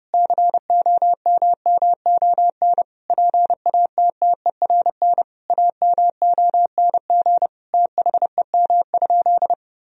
Morse code, heard as a keyed tone, about 32 wpm, 700 Hz. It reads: COMMON PATTERN AMONG THEM?
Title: Sentences from Top 1000 Words - Review Part 2 - 30wpm